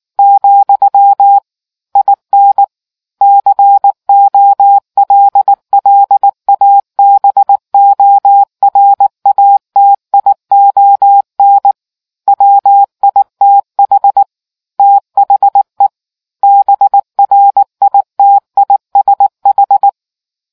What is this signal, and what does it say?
Morse code, heard as a keyed tone, about 19 wpm, 800 Hz. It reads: , IN COLLABORATION WITH THE BRITISH